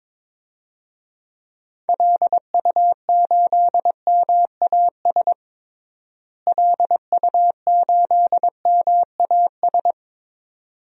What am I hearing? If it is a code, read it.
Morse code, heard as a keyed tone, about 22 wpm, 700 Hz. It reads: LU8MAS LU8MAS
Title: DXCC-100 Call Signs - 6 Characters - Repeated 2x - 22wpm